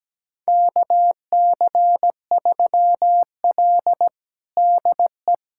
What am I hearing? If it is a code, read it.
Morse code, heard as a keyed tone, about 17 wpm, 700 Hz. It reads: KC3L DE
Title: USA Callsign DE Callsign - 4 Characters - 17wpm